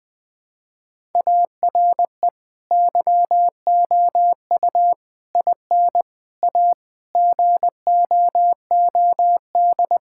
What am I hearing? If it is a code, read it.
Morse code, heard as a keyed tone, about 20 wpm, 700 Hz. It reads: ARE YOU IN A GOOD
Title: Sentences from Top 1000 Words - Review Part 5 - 20wpm